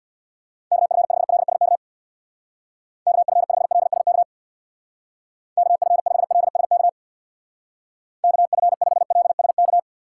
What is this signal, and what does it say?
Morse code, heard as a keyed tone, about 66 wpm, 700 Hz. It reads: KF5LSK KF5LSK KF5LSK KF5LSK